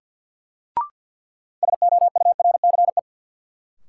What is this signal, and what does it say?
Morse code, heard as a keyed tone, about 52 wpm, 700 Hz. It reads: SOURCE
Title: Sentences from Top 800 Words - New Word Drill - 50wpm